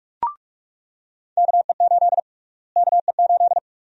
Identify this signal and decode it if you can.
Morse code, heard as a keyed tone, about 43 wpm, 700 Hz. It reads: KE8 KE8
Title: USA Call Sign Prefixes - Repeated 2x - 45wpm